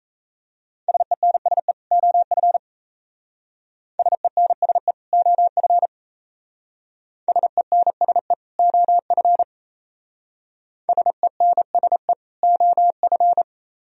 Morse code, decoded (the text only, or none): SENSE OF SENSE OF SENSE OF SENSE OF